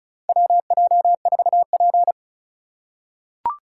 Morse code, heard as a keyed tone, about 34 wpm, 700 Hz. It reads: WJ4P
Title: USA Call Signs - 4 Characters - 6x Character Spacing - 35wpm